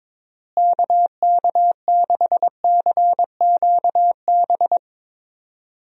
Morse code, heard as a keyed tone, about 22 wpm, 700 Hz. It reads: KK6CQB
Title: USA Call Signs - 6 Characters - Repeated 2x - 22wpm